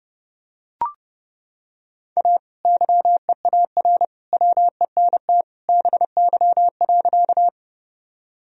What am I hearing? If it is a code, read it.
Morse code, heard as a keyed tone, about 30 wpm, 700 Hz. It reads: A YEAR WENT BY.
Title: Sentences from Top 1000 Words - Review Part 1 - 30wpm